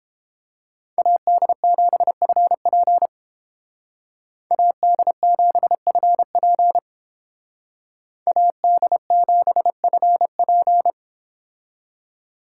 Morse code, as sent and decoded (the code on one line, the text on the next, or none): .- -.. --... ..-. .--. / .- -.. --... ..-. .--. / .- -.. --... ..-. .--.
AD7FP AD7FP AD7FP